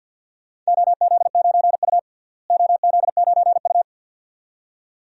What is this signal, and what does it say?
Morse code, heard as a keyed tone, about 52 wpm, 700 Hz. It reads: OZ9U OZ9U
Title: DXCC-100 Call Signs - 4 Characters - Repeated 2x - 50wpm